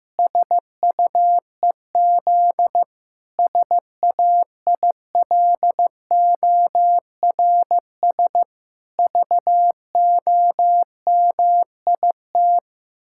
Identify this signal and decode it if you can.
Morse code, heard as a keyed tone, about 15 wpm, 700 Hz. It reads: SUEZ SAILORS VOMIT
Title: Pangram Challenge v2 - 15wpm